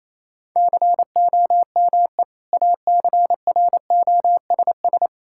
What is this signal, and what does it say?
Morse code, heard as a keyed tone, about 28 wpm, 700 Hz. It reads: COME ACROSS
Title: Sentences from Top 1000 Words - Review Part 5 - 28wpm